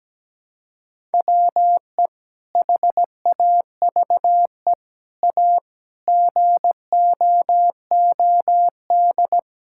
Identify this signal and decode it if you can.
Morse code, heard as a keyed tone, about 17 wpm, 700 Hz. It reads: WE HAVE A GOOD
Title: Sentences from Top 1000 Words - Review Part 5 - 17wpm